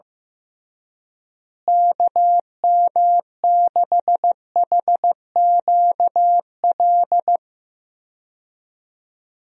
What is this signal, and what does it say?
Morse code, heard as a keyed tone, about 15 wpm, 700 Hz. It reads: KM6HQL
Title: USA Call Signs - 6 Characters - Repeated 3x - 15wpm